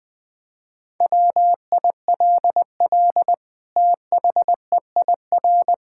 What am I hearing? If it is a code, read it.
Morse code, heard as a keyed tone, about 20 wpm, 700 Hz. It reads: WILL THEIR
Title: Sentences from Top 1000 Words - Review Part 6 - 20wpm